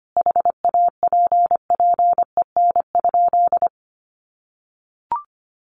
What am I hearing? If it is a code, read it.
Morse code, heard as a keyed tone, about 25 wpm, 700 Hz. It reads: HAPPEN?
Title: Sentences from Top 400 Words 25wpm